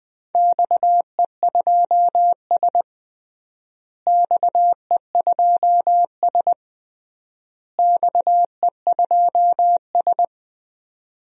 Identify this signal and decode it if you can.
Morse code, heard as a keyed tone, about 20 wpm, 700 Hz. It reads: XE2S XE2S XE2S
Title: DXCC-100 Call Signs - 4 Characters - Repeated 3x - 20wpm